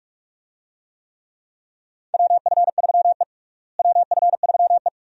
Morse code, heard as a keyed tone, about 46 wpm, 700 Hz. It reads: WF3E WF3E
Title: USA Call Signs - 4 Characters - Repeated 2x - 45wpm